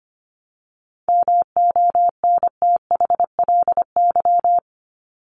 Morse code, heard as a keyed tone, about 25 wpm, 700 Hz. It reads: MONTHLY
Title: LY Suffix - Top 100 Words with Base Words 25wpm